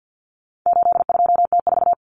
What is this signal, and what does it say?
Morse code, heard as a keyed tone, about 50 wpm, 700 Hz. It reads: 82T4
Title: Common Cut Numbers - 4-Digit 50wpm